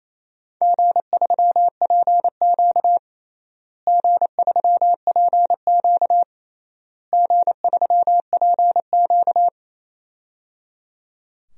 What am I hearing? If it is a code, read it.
Morse code, heard as a keyed tone, about 28 wpm, 700 Hz. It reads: G3PQ G3PQ G3PQ
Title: DXCC-100 Call Signs - 4 Characters - Repeated 3x - 28wpm